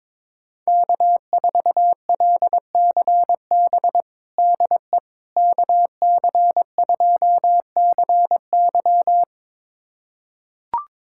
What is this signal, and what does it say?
Morse code, heard as a keyed tone, about 22 wpm, 700 Hz. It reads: K4LCB DE KC2CY
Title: USA Callsign DE Callsign - 5 Characters - 22wpm